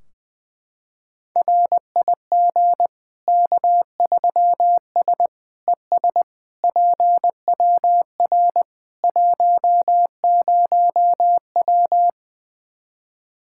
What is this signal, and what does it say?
Morse code, heard as a keyed tone, about 21 wpm, 700 Hz. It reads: RIG K3S ES PWR 10W
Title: Practice QSO - Exchange 2 - Info - 20wpm